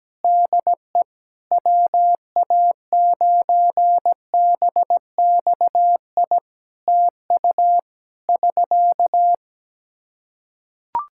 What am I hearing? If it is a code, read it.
Morse code, heard as a keyed tone, about 17 wpm, 700 Hz. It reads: DE WA9BXI TU <SK>
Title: Practice QSO - Exchange 4 - Ending - 17wpm